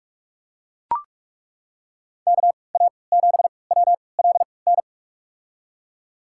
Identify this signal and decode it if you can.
Morse code, heard as a keyed tone, about 43 wpm, spaced as for 25 wpm, 700 Hz. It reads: KA7WLN
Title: USA Call Signs - 3x Character Spacing - 45wpm